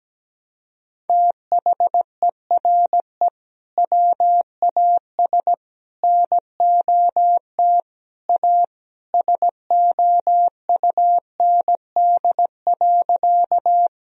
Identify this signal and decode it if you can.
Morse code, heard as a keyed tone, about 17 wpm, 700 Hz. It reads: THERE WAS NOT A SOUND.
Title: Sentences from Top 1000 Words - Review Part 1 - 17wpm